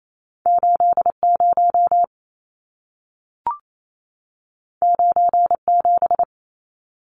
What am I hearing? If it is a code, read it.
Morse code, heard as a keyed tone, about 28 wpm, 700 Hz. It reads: 80 97
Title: Numbers - 2-Digit 28wpm